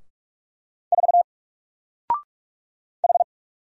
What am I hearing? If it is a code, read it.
Morse code, heard as a keyed tone, about 45 wpm, 700 Hz. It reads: 4 H
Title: Single Character (Letter:Number:Punctuation:Prosign) Alt 45wpm